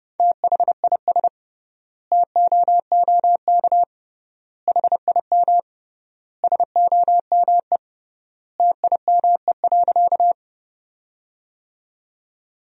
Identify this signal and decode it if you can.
Morse code, heard as a keyed tone, about 30 wpm, 700 Hz. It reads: THIS TOOK HIM SOME TIME.
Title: 3x Word-Spacing - Sentences from Top 300 Words - 30wpm